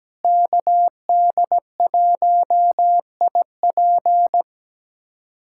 Morse code, heard as a keyed tone, about 17 wpm, 700 Hz. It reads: KD1IP
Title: USA Callsign DE Callsign - 5 Characters - 17wpm